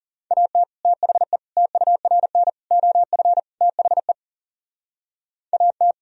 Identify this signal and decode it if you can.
Morse code, heard as a keyed tone, about 34 wpm, 700 Hz. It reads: AT THE TURN OF THE AT
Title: Speed-Racing - Sets of 5 Words 30wpm